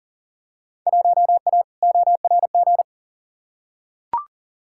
Morse code, heard as a keyed tone, about 40 wpm, 700 Hz. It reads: 1A ORG
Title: Field Day Exchanges Encore - 40wpm